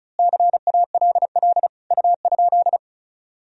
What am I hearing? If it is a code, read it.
Morse code, heard as a keyed tone, about 34 wpm, 700 Hz. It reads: CALL U?
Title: Sentences from Top 100 Words and Abbreviations 35wpm